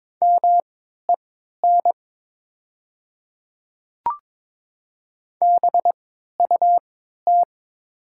Morse code, heard as a keyed tone, about 22 wpm, spaced as for 13 wpm, 700 Hz. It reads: MEN BUT
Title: 3x Character Spacing - Top 200 Words - 22wpm